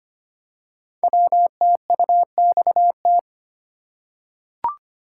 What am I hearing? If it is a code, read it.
Morse code, heard as a keyed tone, about 25 wpm, 700 Hz. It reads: WTUXT
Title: Code Groups - 5 Letters - 25wpm